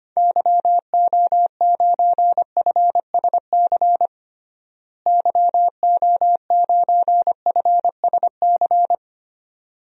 Morse code, heard as a keyed tone, about 25 wpm, 700 Hz. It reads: YO9FSC YO9FSC
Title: DXCC-100 Call Signs - 6 Characters - Repeated 2x - 25wpm